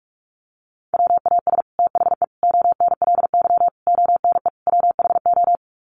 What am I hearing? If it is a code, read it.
Morse code, heard as a keyed tone, about 44 wpm, 700 Hz. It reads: WAS THE ONLY ONE WHO
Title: Sets of 5 Words Encore2 45wpm